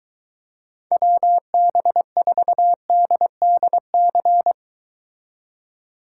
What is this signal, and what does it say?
Morse code, heard as a keyed tone, about 23 wpm, 700 Hz. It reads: WB4DDC